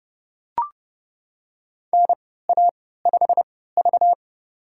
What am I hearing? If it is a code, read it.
Morse code, heard as a keyed tone, about 31 wpm, spaced as for 17 wpm, 700 Hz. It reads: NA5V
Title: USA Call Signs - 4 Characters - 3x Character Spacing - 30wpm